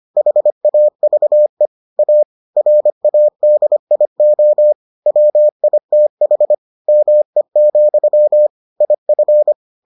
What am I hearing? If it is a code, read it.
Morse code, heard as a keyed tone, about 25 wpm, 600 Hz. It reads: HAVE A RADIO WITH ME, IF